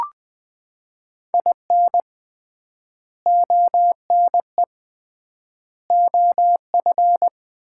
Morse code, heard as a keyed tone, about 20 wpm, 700 Hz. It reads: IN ONE OF
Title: 3x Word-Spacing - Sets of 3 Words - 20wpm